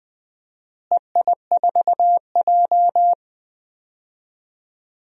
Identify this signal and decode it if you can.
Morse code, heard as a keyed tone, about 20 wpm, 700 Hz. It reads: EI4J
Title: DXCC-100 Call Signs - 4 Characters - Repeated 3x - 20wpm